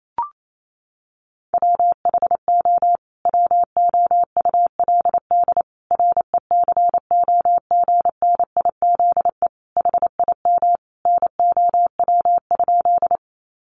Morse code, heard as a keyed tone, about 28 wpm, 700 Hz. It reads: WHO WOULD RECOGNIZE HIM NOW?
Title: Sentences from Top 1400 Words 28wpm